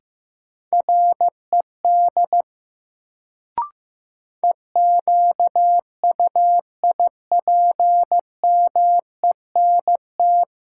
Morse code, heard as a keyed tone, about 15 wpm, 700 Hz. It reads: RED EQUIPMENT
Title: Sentences from Top 700 Words - New Word Drill - 15wpm